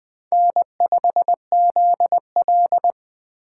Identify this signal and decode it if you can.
Morse code, heard as a keyed tone, about 21 wpm, 700 Hz. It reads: N5ZL